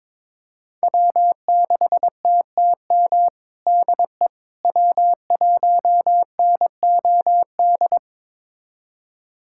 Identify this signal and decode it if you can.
Morse code, heard as a keyed tone, about 22 wpm, 700 Hz. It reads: W6TTM DE W1NOD
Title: USA Callsign DE Callsign - 5 Characters - 22wpm